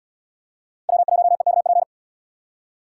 Morse code, heard as a keyed tone, about 66 wpm, 700 Hz. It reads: W1EKP